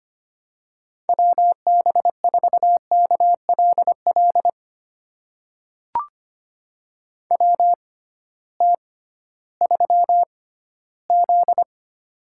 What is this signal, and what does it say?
Morse code, heard as a keyed tone, about 25 wpm, spaced as for 15 wpm, 700 Hz. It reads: WB4KLL WT3Z
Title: USA Call Signs - 6x Character Spacing - 25wpm